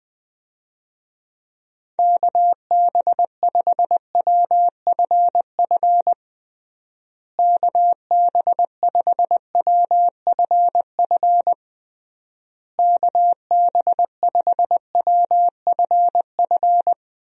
Morse code, heard as a keyed tone, about 21 wpm, 700 Hz. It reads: KB5WFF KB5WFF KB5WFF
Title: USA Call Signs - 6 Characters - Repeated 3x - 20wpm